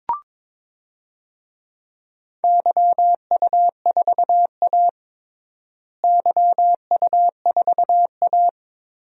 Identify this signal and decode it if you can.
Morse code, heard as a keyed tone, about 22 wpm, 700 Hz. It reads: YU4A YU4A
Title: DXCC-100 Call Signs - 4 Characters - Repeated 3x - 22wpm